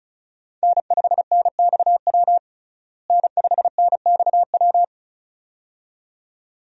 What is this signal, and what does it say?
Morse code, heard as a keyed tone, about 36 wpm, 700 Hz. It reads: N5NXW N5NXW
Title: USA Call Signs - 5 Characters - Repeated 2x - 35wpm